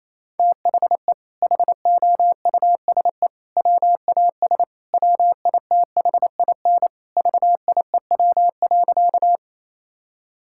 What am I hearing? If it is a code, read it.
Morse code, heard as a keyed tone, about 27 wpm, 700 Hz. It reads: THE HOUSE WAS WITHIN VIEW.
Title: Sentences from Top 1000 Words - Review Part 6 - 28wpm